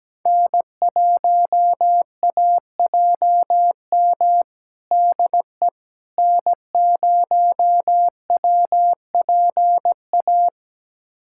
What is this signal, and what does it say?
Morse code, heard as a keyed tone, about 17 wpm, 700 Hz. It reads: N1AJM DE N0WPA
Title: USA Callsign DE Callsign 17wpm